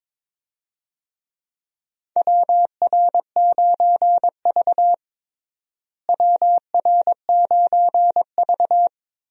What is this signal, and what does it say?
Morse code, heard as a keyed tone, about 21 wpm, 700 Hz. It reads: WR9V WR9V
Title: USA Call Signs - 4 Characters - Repeated 3x - 22wpm